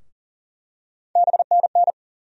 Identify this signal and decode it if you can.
Morse code, heard as a keyed tone, about 40 wpm, 700 Hz. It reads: BNN
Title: USA Call Sign Postfixes - 3 Letters 40wpm